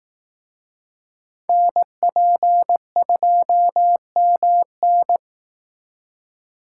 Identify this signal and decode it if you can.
Morse code, heard as a keyed tone, about 18 wpm, 700 Hz. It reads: NP2MN
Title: Speed Racing - USA Call Signs 15wpm